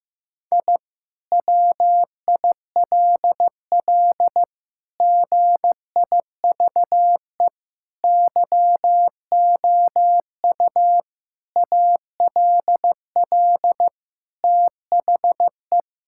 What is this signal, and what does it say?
Morse code, heard as a keyed tone, about 15 wpm, 700 Hz. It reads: I WILL GIVE YOU ALL THE
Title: Sentences from Top 1000 Words - Review Part 6 - 15wpm